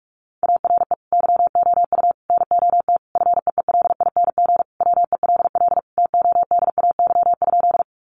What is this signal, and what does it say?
Morse code, heard as a keyed tone, about 44 wpm, 700 Hz. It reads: ARE YOU NOT FEELING WELL TODAY?
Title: Sentences from Top 1200 Words 45wpm